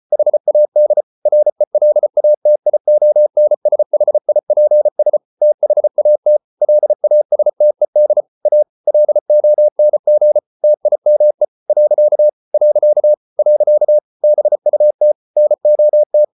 Morse code, heard as a keyed tone, about 34 wpm, 600 Hz. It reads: HAD RELATIONSHIPS THAT LASTED A LONG TIME . . BUT NOT